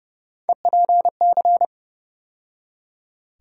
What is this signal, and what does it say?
Morse code, heard as a keyed tone, about 32 wpm, 700 Hz. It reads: EPC
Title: USA Call Sign Postfixes - 3 Letters 30wpm